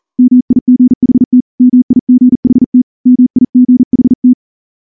Morse code, heard as a keyed tone, about 38 wpm, 250 Hz. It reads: MIGHT MIGHT MIGHT
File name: morse / morse-training-40WPM